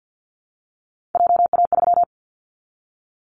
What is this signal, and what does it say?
Morse code, heard as a keyed tone, about 50 wpm, 700 Hz. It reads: JA3
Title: Callsign Prefixes for Top 25 DXCC - Repeated 3x - 50wpm